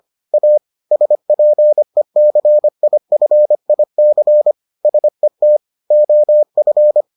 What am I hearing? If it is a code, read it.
Morse code, heard as a keyed tone, about 25 wpm, 600 Hz. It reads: A SPECIFIC SET OF